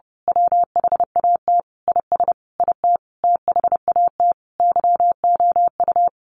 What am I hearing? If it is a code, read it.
Morse code, heard as a keyed tone, about 30 wpm, 700 Hz. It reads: WHAT IS IT THAT YOU
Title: Sentences from Top 200 Words 30wpm